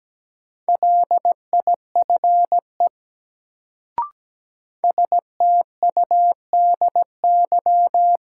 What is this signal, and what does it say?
Morse code, heard as a keyed tone, about 17 wpm, 700 Hz. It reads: LIFE STUDY
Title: Sentences from Top 300 Words - New Word Drill - 17wpm